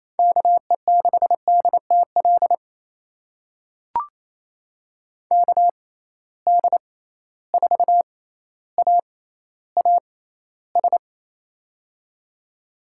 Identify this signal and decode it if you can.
Morse code, heard as a keyed tone, about 28 wpm, spaced as for 14 wpm, 700 Hz. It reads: KE6DTL KD4AAS
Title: USA Call Signs - 6 Characters - 6x Character Spacing - 28wpm